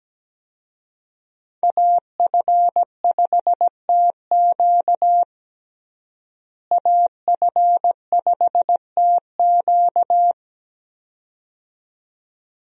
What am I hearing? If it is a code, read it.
Morse code, heard as a keyed tone, about 17 wpm, 700 Hz. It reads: AF5TQ AF5TQ
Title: USA Call Signs - 5 Characters - Repeated 2x - 17wpm